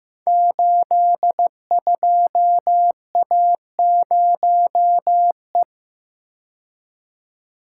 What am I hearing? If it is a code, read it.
Morse code, heard as a keyed tone, about 15 wpm, 700 Hz. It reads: 82A0E
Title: Code Groups - 5 Letter:Numbers - 15wpm